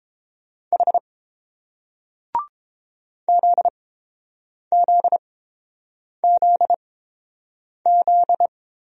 Morse code, heard as a keyed tone, about 30 wpm, spaced as for 7 wpm, 700 Hz. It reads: H ZZZZ